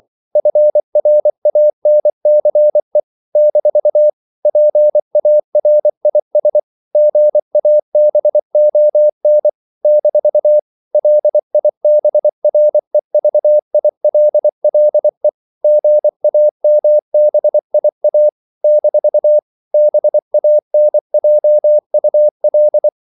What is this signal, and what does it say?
Morse code, heard as a keyed tone, about 24 wpm, 600 Hz. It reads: FRANCE - PARIS GABON - LIBREVILLE GAMBIA - BANJUL
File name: CW / Practice / NationalCapitals24